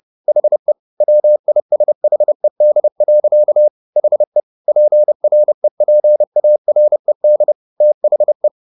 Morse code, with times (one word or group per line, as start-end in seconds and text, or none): 0.3-0.7 HE
1.0-3.7 WISHED.
4.0-4.4 HE
4.7-7.5 PREPARED
7.8-8.5 THE